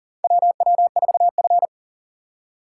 Morse code, heard as a keyed tone, about 40 wpm, 700 Hz. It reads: WW4F
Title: USA Call Signs - 3x Character Spacing - 40wpm